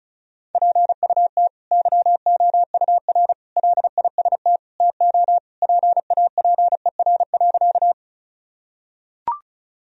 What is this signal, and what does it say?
Morse code, heard as a keyed tone, about 36 wpm, 700 Hz. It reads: PUT YOUR LIST TO PAPER.
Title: Sentences from Top 1000 Words - Review Part 2 - 35wpm